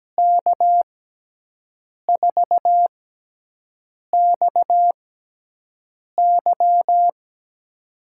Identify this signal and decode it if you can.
Morse code, heard as a keyed tone, about 17 wpm, 700 Hz. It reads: K 4 X Y
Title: USA Call Signs - 5 Characters - 6x Character Spacing - 17wpm